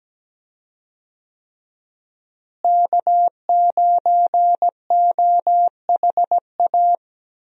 Morse code, heard as a keyed tone, about 17 wpm, 700 Hz. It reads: K9OHA
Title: USA Call Signs - 5 Characters - Repeated 3x - 17wpm